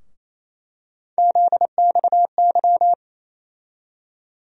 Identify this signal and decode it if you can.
Morse code, heard as a keyed tone, about 28 wpm, 700 Hz. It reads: ZXY
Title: USA Call Sign Postfixes - 3 Letters 28wpm